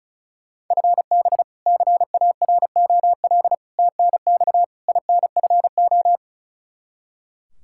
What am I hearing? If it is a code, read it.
Morse code, heard as a keyed tone, about 36 wpm, 700 Hz. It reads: FB CAROL TNX INFO
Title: K1USN SST QSO Practice 35wpm